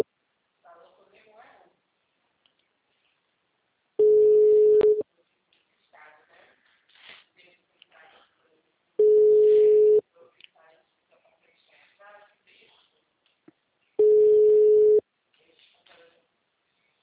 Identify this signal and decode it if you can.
Ringback tone (Europe, 425 Hz), with speech.
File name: out- -2014- - - .16211